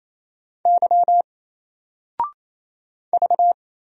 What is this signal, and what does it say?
Morse code, heard as a keyed tone, about 27 wpm, 700 Hz. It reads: Y V
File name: Single-Letter-Number-28wpm